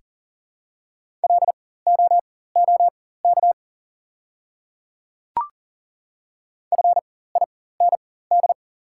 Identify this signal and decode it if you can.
Morse code, heard as a keyed tone, about 37 wpm, spaced as for 18 wpm, 700 Hz. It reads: LOOK FIND
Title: 4x Character Spacing - Top 100 Words - 40wpm